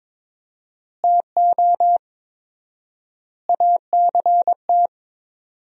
Morse code, heard as a keyed tone, about 22 wpm, 700 Hz. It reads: TO ACT
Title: 4x Word-Spacing - Sentences from Top 500 Words - 22wpm